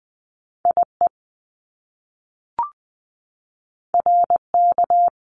Morse code, heard as a keyed tone, about 20 wpm, 700 Hz. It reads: IE RK
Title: Top Two Letter Combinations in Words 20wpm